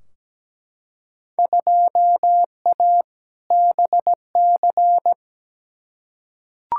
Morse code, heard as a keyed tone, about 17 wpm, 700 Hz. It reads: 2A BC
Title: Field Day Exchanges Encore - 17wpm